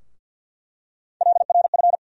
Morse code, heard as a keyed tone, about 47 wpm, 700 Hz. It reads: LRF
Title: USA Call Sign Postfixes - 3 Letters 50wpm